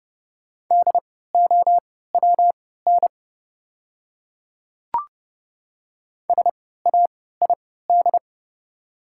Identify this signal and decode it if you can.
Morse code, heard as a keyed tone, about 30 wpm, spaced as for 17 wpm, 690 Hz. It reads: DOWN SAID
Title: 3x Character Spacing - Top 200 Words - 30wpm